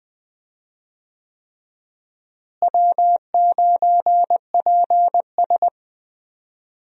Morse code, heard as a keyed tone, about 21 wpm, 700 Hz. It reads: W9PS
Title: USA Call Signs - 4 Characters - Repeated 3x - 20wpm